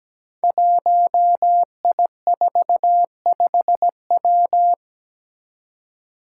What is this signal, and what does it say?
Morse code, heard as a keyed tone, about 17 wpm, 700 Hz. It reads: 1I45W
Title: Five Challenging Letter-Number Combinations 17wpm